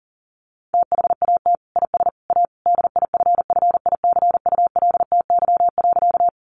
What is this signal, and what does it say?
Morse code, heard as a keyed tone, about 40 wpm, 700 Hz. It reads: THAT IS A DIFFICULTY.
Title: Sentences from Top 1500 Words 40wpm